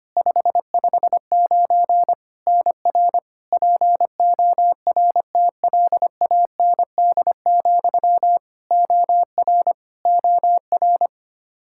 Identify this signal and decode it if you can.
Morse code, heard as a keyed tone, about 25 wpm, 700 Hz. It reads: 559 NR PORTLAND, OR OR
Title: Practice QSO - Exchange 1 - Beginning - 25wpm